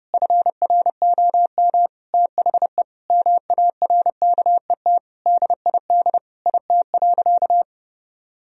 Morse code, heard as a keyed tone, about 30 wpm, 700 Hz. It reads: FROM THE MARKET DID IT.
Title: Sentences from Top 1000 Words - Review Part 3 - 30wpm